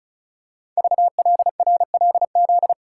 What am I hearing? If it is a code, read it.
Morse code, heard as a keyed tone, about 36 wpm, 700 Hz. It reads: VLRLZ
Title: Code Groups - 5 Letters - 35wpm